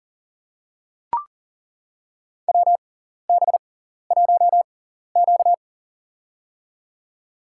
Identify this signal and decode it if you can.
Morse code, heard as a keyed tone, about 39 wpm, spaced as for 14 wpm, 700 Hz. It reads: WB1Q
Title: USA Call Signs - 6x Character Spacing - 40wpm